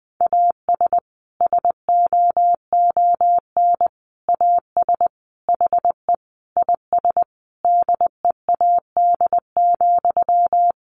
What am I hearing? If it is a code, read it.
Morse code, heard as a keyed tone, about 20 wpm, 700 Hz. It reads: AS SOON AS HE IS DEAD,
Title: Sentences from Top 1500 Words 20wpm